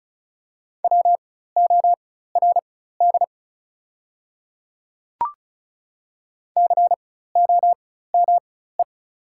Morse code, heard as a keyed tone, about 34 wpm, spaced as for 16 wpm, 700 Hz. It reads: WORD COME
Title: 4x Character Spacing - Top 100 Words - 35wpm